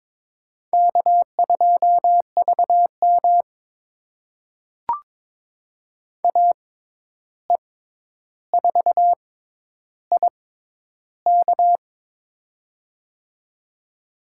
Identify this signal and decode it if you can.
Morse code, heard as a keyed tone, about 22 wpm, spaced as for 11 wpm, 700 Hz. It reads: K2VM AE4IK
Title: USA Call Signs - 6x Character Spacing - 22wpm